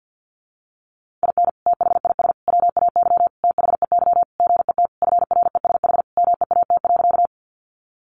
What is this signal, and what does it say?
Morse code, heard as a keyed tone, about 50 wpm, 700 Hz. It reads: IN THIS WAY THEY GET FRESH MEAT.
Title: Sentences from Top 1200 Words 50wpm